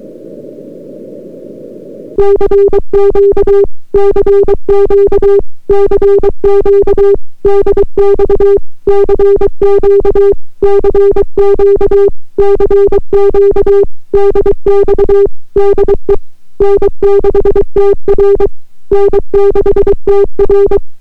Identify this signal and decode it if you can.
Morse code, heard as a keyed tone, about 23 wpm, 400 Hz. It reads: CQ CQ CQ DX CQ CQ CQ DX DE N6TR N6TR